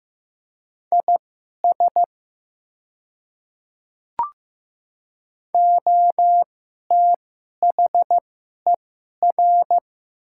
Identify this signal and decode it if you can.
Morse code, heard as a keyed tone, about 15 wpm, spaced as for 11 wpm, 700 Hz. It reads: IS OTHER
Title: 2x Character Spacing - Top 100 Words - 15wpm